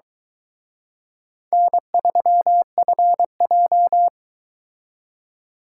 Morse code, heard as a keyed tone, about 23 wpm, 700 Hz. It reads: N3FJ